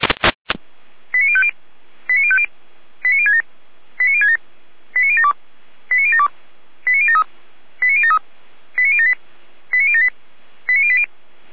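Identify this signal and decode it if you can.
Five-tone selective call (ZVEI), heard as quick runs of five tones, tones of about 70 ms. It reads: 89050 89050 89067 89067 89072 89072 89073 89073 89078 89078 89080
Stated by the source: ZVEI five-tone selective call received on an SDR, 455.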